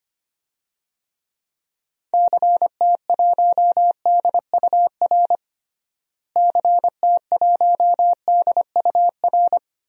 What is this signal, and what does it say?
Morse code, heard as a keyed tone, about 25 wpm, 700 Hz. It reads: CT1DUR CT1DUR
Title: DXCC-100 Call Signs - 6 Characters - Repeated 3x - 25wpm